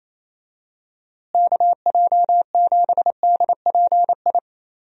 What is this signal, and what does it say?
Morse code, heard as a keyed tone, about 28 wpm, 700 Hz. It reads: KJ7DPI
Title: USA Call Signs - 6 Characters - Repeated 3x - 28wpm